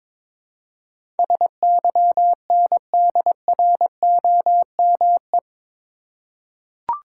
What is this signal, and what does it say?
Morse code, heard as a keyed tone, about 22 wpm, 700 Hz. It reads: SYNDROME
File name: practice / 22wpm / Eight Letter Words - 22wpm